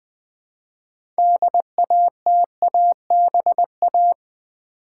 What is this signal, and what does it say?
Morse code, heard as a keyed tone, about 20 wpm, 700 Hz. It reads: DATABA
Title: Build A Word - 2 Letters At A Time - 8-Letter Words - 20wpm